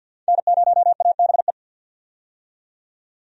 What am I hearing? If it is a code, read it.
Morse code, heard as a keyed tone, about 51 wpm, 700 Hz. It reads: N0ABE
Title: USA Call Signs - 5 Characters - 50wpm